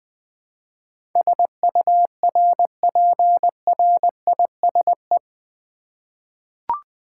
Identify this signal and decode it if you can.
Morse code, heard as a keyed tone, about 20 wpm, 700 Hz. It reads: SURPRISE
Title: 8 Letter Words - First 3 Letters Given - 20wpm